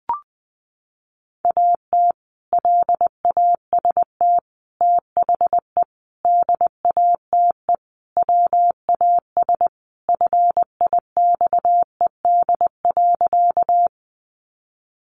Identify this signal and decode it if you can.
Morse code, heard as a keyed tone, about 20 wpm, 700 Hz. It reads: AT LAST THE DATE WAS FIXED.
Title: Sentences from Top 1100 Words 20wpm